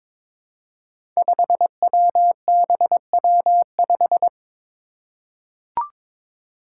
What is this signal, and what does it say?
Morse code, heard as a keyed tone, about 22 wpm, 700 Hz. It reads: 5WBW5
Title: Five Challenging Letter-Number Combinations 22wpm